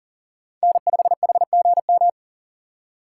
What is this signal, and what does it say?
Morse code, heard as a keyed tone, about 40 wpm, 700 Hz. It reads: N5HGM
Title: USA Call Signs - 5 Characters - 4x Character Spacing - 40wpm